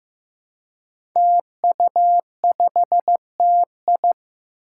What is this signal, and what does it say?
Morse code, heard as a keyed tone, about 16 wpm, 700 Hz. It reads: TU5TI
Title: Five Challenging Letter-Number Combinations 15wpm